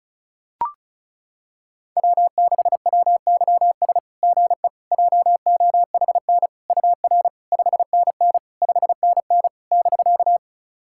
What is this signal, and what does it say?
Morse code, heard as a keyed tone, about 36 wpm, 690 Hz. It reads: W6WYS GE JOHN UR 5NN 5NN <BK>
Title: SOTA QSO Practice 35wpm